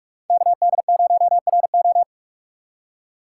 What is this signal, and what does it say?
Morse code, heard as a keyed tone, about 43 wpm, 700 Hz. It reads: KD0RO
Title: USA Call Signs - 5 Characters - 4x Character Spacing - 45wpm